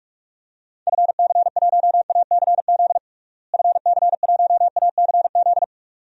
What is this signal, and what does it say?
Morse code, heard as a keyed tone, about 46 wpm, 700 Hz. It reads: FC1ACZ FC1ACZ
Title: DXCC-100 Call Signs - 6 Characters - Repeated 2x - 45wpm